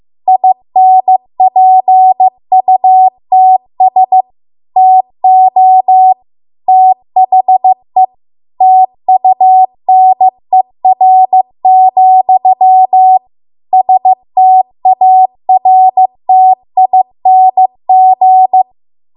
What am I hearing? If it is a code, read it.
Morse code, heard as a keyed tone, about 15 wpm, 750 Hz. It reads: INPUTS TO THE TUNER, STARTING
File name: Archive / 15 WPM / 15WPM